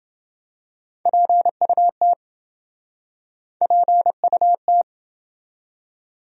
Morse code, heard as a keyed tone, about 27 wpm, 700 Hz. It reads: PUT PUT